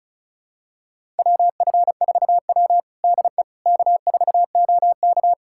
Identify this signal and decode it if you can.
Morse code, heard as a keyed tone, about 36 wpm, 700 Hz. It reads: WF4W DE K4OK
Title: USA Callsign DE Callsign - 4 Characters - 35wpm